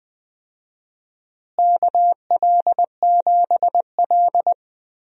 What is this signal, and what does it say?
Morse code, heard as a keyed tone, about 20 wpm, 700 Hz. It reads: KL7L
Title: DXCC-100 Call Signs - 4 Characters - Repeated 3x - 20wpm